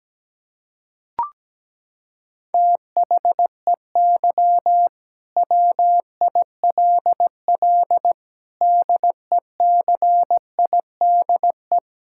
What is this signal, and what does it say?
Morse code, heard as a keyed tone, about 17 wpm, 700 Hz. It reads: THEY WILL DECIDE
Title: Sentences from Top 1600 Words 17wpm